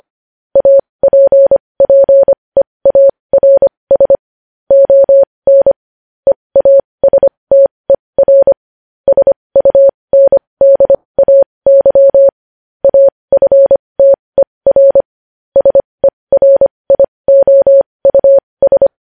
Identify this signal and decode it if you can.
Morse code, heard as a keyed tone, about 25 wpm, spaced as for 20 wpm, 550 Hz. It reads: APPEARS ON EASTER SUNDAY AFTER SERIOUS
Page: news-headlines-in-morse-code-at-20-wpm